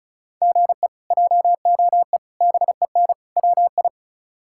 Morse code, heard as a keyed tone, about 36 wpm, 700 Hz. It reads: GE JOE BEN WI
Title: K1USN SST QSO Practice - 35wpm